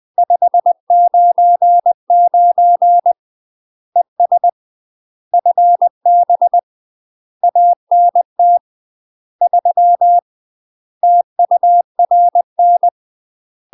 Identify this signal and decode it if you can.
Morse code, heard as a keyed tone, about 20 wpm, 700 Hz. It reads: 599 ES FB ANT 3 TURN
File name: UK MASTER QSOs TRAINING/20WPM